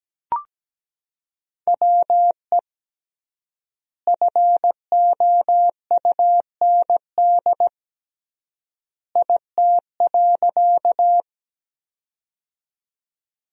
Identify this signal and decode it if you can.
Morse code, heard as a keyed tone, about 17 wpm, 700 Hz. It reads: WE FOUND IT.
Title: 3x Word-Spacing - Sentences from Top 300 Words - 17wpm